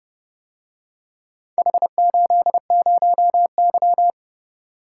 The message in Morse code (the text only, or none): H80Y